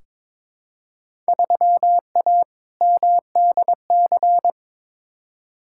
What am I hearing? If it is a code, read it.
Morse code, heard as a keyed tone, about 22 wpm, 700 Hz. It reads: 3A MDC
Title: Field Day Exchanges - 22wpm